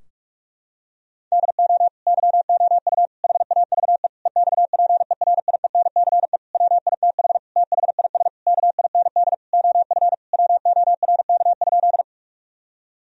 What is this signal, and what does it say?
Morse code, heard as a keyed tone, about 46 wpm, 700 Hz. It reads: DO YOU HAVE EXPERIENCE WITH THIS KIND OF WORK?
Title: Sentences from Top 1000 Words - Review Part 3 - 45wpm